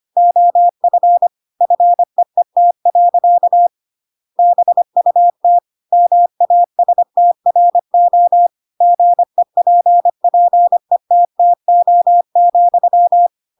Morse code, heard as a keyed tone, about 25 wpm, 700 Hz. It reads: OF FEET. BUT MASTRO GEPPETTO,
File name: Pinocchio / Pinocchio ch8 25 25 ik1hge